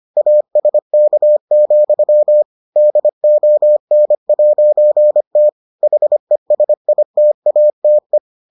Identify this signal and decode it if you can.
Morse code, heard as a keyed tone, about 25 wpm, 600 Hz. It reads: ASK, DON'T HESITATE